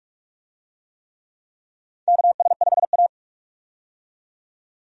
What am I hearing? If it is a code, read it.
Morse code, heard as a keyed tone, about 41 wpm, 700 Hz. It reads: KS5A
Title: USA Call Signs - 4 Characters - 45wpm